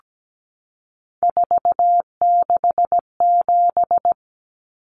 Morse code, heard as a keyed tone, about 17 wpm, 700 Hz. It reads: 467
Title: Numbers - 3-Digit 17wpm